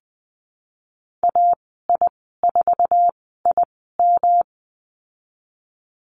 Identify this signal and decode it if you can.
Morse code, heard as a keyed tone, about 20 wpm, spaced as for 15 wpm, 700 Hz. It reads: AI4IM
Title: 161 Lesson 26g - Letter g - Call Signs 20wpm